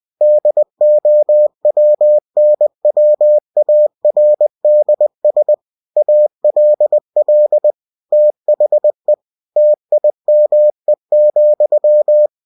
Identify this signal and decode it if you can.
Morse code, heard as a keyed tone, about 20 wpm, 600 Hz. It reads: DOWNWARDS ALL THE TIME,
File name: cw-books / winnie-the-pooh / winnie-20wpm-page-0053